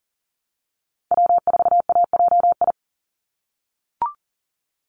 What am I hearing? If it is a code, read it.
Morse code, heard as a keyed tone, about 40 wpm, 700 Hz. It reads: W4AJI
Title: USA Call Signs 40wpm